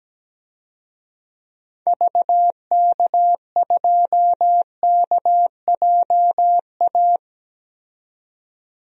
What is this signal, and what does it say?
Morse code, heard as a keyed tone, about 17 wpm, 700 Hz. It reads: VK2KJA
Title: DXCC-100 Call Signs - 6 Characters - 17wpm